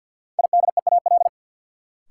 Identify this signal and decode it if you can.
Morse code, heard as a keyed tone, about 53 wpm, 700 Hz. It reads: IDEAL
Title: Sentences from Top 1400 Words - New Word Drill - 50wpm